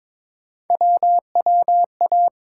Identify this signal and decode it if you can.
Morse code, heard as a keyed tone, about 22 wpm, 700 Hz. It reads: WWA
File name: ARRL Sections v2 - 22wpm